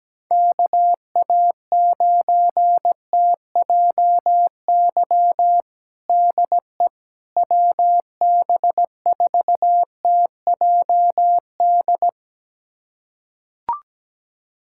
Morse code, heard as a keyed tone, about 17 wpm, 700 Hz. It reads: KA9TJY DE WB4TJD
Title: USA Callsign DE Callsign 17wpm